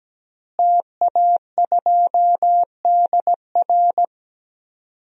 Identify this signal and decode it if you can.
Morse code, heard as a keyed tone, about 17 wpm, 700 Hz. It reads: TA2DR
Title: DXCC-100 Call Signs - 5 Characters - Repeated 2x - 17wpm